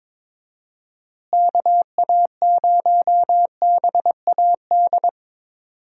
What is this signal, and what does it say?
Morse code, heard as a keyed tone, about 22 wpm, 700 Hz. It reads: KA0BAD
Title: USA Call Signs - 6 Characters - Repeated 2x - 22wpm